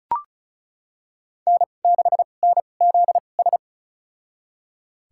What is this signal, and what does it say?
Morse code, heard as a keyed tone, about 34 wpm, spaced as for 25 wpm, 700 Hz. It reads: N6NZS
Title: USA Call Signs - 5 Characters - 2x Character Spacing - 35wpm